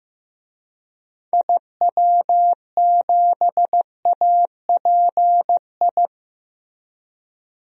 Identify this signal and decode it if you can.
Morse code, heard as a keyed tone, about 15 wpm, 700 Hz. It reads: IW7API
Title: DXCC-100 Call Signs - 6 Characters - Repeated 3x - 15wpm